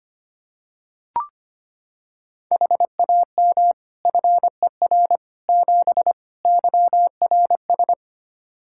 Morse code, heard as a keyed tone, about 25 wpm, 700 Hz. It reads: HAM FER 7 YRS
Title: Practice QSO - Exchange 3 - More Info - 25wpm